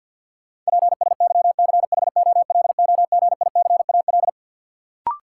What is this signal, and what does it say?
Morse code, heard as a keyed tone, about 52 wpm, 700 Hz. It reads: PSYCHOLOGICAL
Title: Sentences from Top 1400 Words - New Word Drill - 50wpm